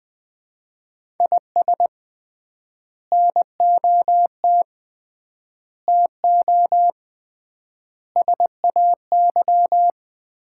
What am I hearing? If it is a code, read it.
Morse code, heard as a keyed tone, about 20 wpm, 700 Hz. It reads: IS NOT TO SAY
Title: 3x Word-Spacing - Sets of 5 Words - 20wpm